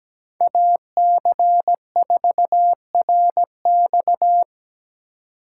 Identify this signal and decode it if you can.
Morse code, heard as a keyed tone, about 17 wpm, 700 Hz. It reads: AC4RX
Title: USA Call Signs - 8x Character Spacing - 17wpm